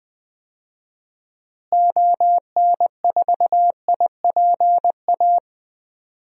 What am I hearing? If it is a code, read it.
Morse code, heard as a keyed tone, about 20 wpm, 700 Hz. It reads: ON4IPA
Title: DXCC-100 Call Signs - 6 Characters - Repeated 3x - 20wpm